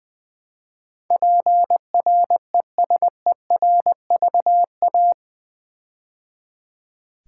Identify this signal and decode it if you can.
Morse code, heard as a keyed tone, about 20 wpm, 700 Hz. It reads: PRESERVA
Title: Build A Word - 4 Letters At A Time - 12-Letter Words - 20wpm